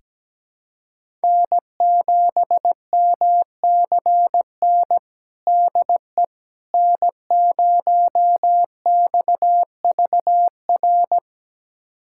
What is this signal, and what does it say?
Morse code, heard as a keyed tone, about 17 wpm, 700 Hz. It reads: N7MCN DE N0XVR
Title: USA Callsign DE Callsign - 5 Characters - 17wpm